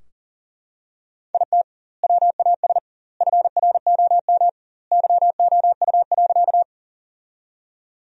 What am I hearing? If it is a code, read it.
Morse code, heard as a keyed tone, about 40 wpm, 700 Hz. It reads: IT WAS FROM YOU.
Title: 2x Word-Spacing - Sentences from Top 100 Words - 40wpm